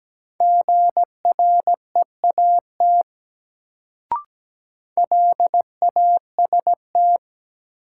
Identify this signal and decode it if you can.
Morse code, heard as a keyed tone, about 17 wpm, 700 Hz. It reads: GREAT LAST
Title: Sets of 2 Words - New Word Drill - 17wpm